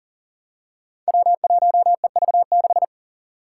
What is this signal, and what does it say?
Morse code, heard as a keyed tone, about 39 wpm, 700 Hz. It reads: W1EV6
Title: Five Challenging Letter-Number Combinations 40wpm